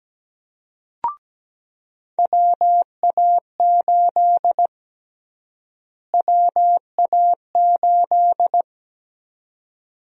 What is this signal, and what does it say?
Morse code, heard as a keyed tone, about 17 wpm, 700 Hz. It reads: WA8 WA8
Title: USA Call Sign Prefixes - Repeated 2x - 17wpm